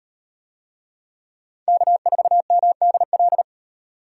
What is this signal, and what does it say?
Morse code, heard as a keyed tone, about 38 wpm, 700 Hz. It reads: K4MDL